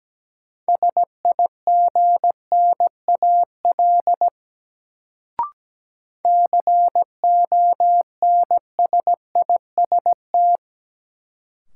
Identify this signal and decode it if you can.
Morse code, heard as a keyed tone, about 17 wpm, 700 Hz. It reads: SIGNAL CONSIST
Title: Sentences from Top 1400 Words - New Word Drill Alt - 17wpm